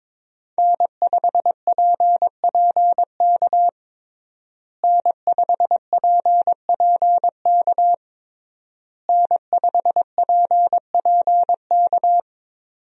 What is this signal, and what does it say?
Morse code, heard as a keyed tone, about 22 wpm, 700 Hz. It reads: N5PPK N5PPK N5PPK
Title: USA Call Signs - 5 Characters - Repeated 3x - 22wpm